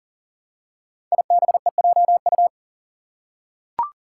Morse code, heard as a keyed tone, about 41 wpm, 700 Hz. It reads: IBEJU
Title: Five Challenging Letter-Number Combinations 40wpm